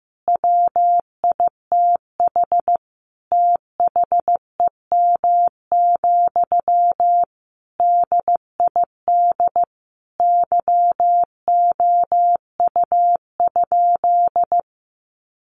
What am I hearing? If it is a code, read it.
Morse code, heard as a keyed tone, about 15 wpm, 700 Hz. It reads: WITH THEM, DID YOU?
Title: Sentences from Top 1500 Words 15wpm